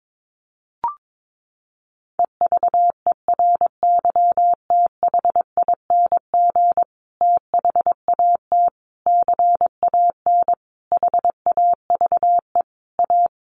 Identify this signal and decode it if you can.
Morse code, heard as a keyed tone, about 22 wpm, 700 Hz. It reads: EVERYTHING THAT CAN HAVE A
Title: Sentences from Top 1300 Words 22wpm